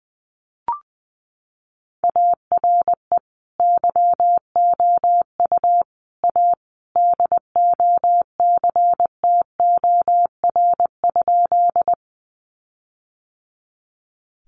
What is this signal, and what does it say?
Morse code, heard as a keyed tone, about 20 wpm, 700 Hz. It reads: ARE YOU A DOCTOR?
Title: Sentences from Top 1500 Words 20wpm